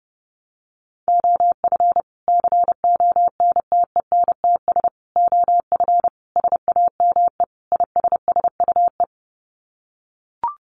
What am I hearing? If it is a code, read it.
Morse code, heard as a keyed tone, about 30 wpm, 700 Hz. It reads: OF CONTENTS OF SAME ISSUE
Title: Sets of 5 Words Encore2 30wpm